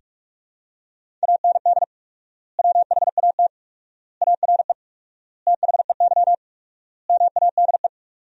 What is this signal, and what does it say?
Morse code, heard as a keyed tone, about 47 wpm, 700 Hz. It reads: AND WHAT ARE THEY MADE
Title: 4x Word-Spacing - Sentences from Top 200 Words - 45wpm